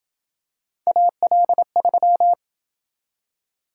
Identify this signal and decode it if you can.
Morse code, heard as a keyed tone, about 27 wpm, 700 Hz. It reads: AL3